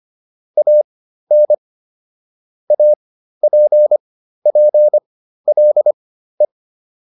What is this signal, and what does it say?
Morse code, heard as a keyed tone, about 25 wpm, spaced as for 13 wpm, 600 Hz. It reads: AN APPLE